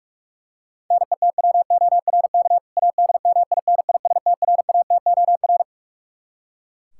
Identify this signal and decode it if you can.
Morse code, heard as a keyed tone, about 41 wpm, 700 Hz. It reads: NETWORK ADMINISTRATOR
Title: K1USN SST QSO Practice 45wpm